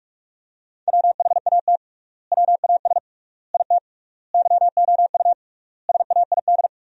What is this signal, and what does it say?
Morse code, heard as a keyed tone, about 44 wpm, 700 Hz. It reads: WHAT WAS IT YOU SAID
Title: 3x Word-Spacing - Sentences from Top 200 Words - 45wpm